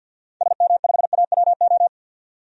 Morse code, heard as a keyed tone, about 52 wpm, 700 Hz. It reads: SM5AWO
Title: DXCC-100 Call Signs - 6 Characters - 50wpm